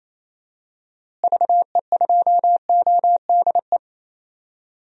Morse code, heard as a keyed tone, about 28 wpm, 700 Hz. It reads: VE2ODE
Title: DXCC-100 Call Signs - 6 Characters - Repeated 2x - 28wpm